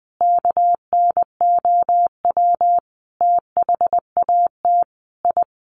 Morse code, heard as a keyed tone, about 20 wpm, 700 Hz. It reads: KNOW THAT I
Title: Sets of 5 Words Encore2 20wpm